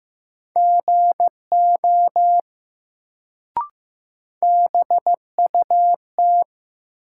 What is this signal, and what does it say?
Morse code, heard as a keyed tone, about 15 wpm, 700 Hz. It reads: GO BUT
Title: Sentences from Top 100 Words - New Word Drill - 15wpm